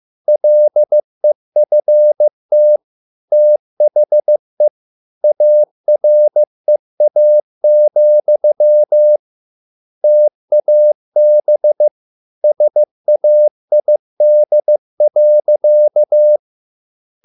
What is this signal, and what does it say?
Morse code, heard as a keyed tone, about 15 wpm, 600 Hz. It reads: LEFT THE AREA, TAB SAID.